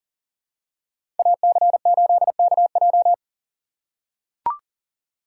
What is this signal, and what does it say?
Morse code, heard as a keyed tone, about 40 wpm, 700 Hz. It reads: AC8KJ
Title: USA Call Signs - 5 Characters - 3x Character Spacing - 40wpm